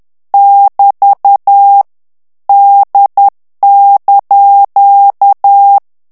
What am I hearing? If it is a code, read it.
Morse code, heard as a keyed tone, about 10 wpm, 800 Hz. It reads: <BT> D)
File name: ITALIEN texte clair 2 vitesse 10 mots minute